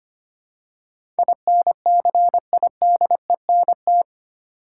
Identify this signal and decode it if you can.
Morse code, heard as a keyed tone, about 25 wpm, 700 Hz. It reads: INCIDENT
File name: Eight Letter Words - 25wpm